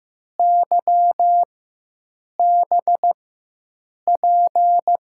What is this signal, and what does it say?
Morse code, heard as a keyed tone, about 15 wpm, spaced as for 7 wpm, 700 Hz. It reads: YBP
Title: USA Call Signs - 6 Characters - 4x Character Spacing - 15wpm